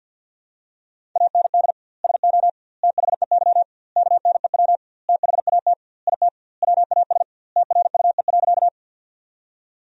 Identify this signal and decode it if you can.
Morse code, heard as a keyed tone, about 52 wpm, 700 Hz. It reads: AND SO THEY KNEW THAT IT WAS TRUE.
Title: 2x Word-Spacing - Sentences from Top 400 Words - 50wpm